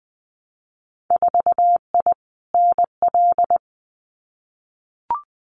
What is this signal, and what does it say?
Morse code, heard as a keyed tone, about 20 wpm, 700 Hz. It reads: 4I NL
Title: Winter Field Day Exchanges Encore 20wpm